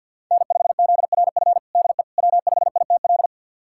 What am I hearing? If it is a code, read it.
Morse code, heard as a keyed tone, about 51 wpm, 700 Hz. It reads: N5ZRF DE W5ITL
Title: USA Callsign DE Callsign - 5 Characters - 50wpm